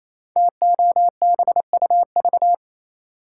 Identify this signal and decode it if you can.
Morse code, heard as a keyed tone, about 28 wpm, 700 Hz. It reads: TOBUV
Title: Code Groups - 5 Letters - 28wpm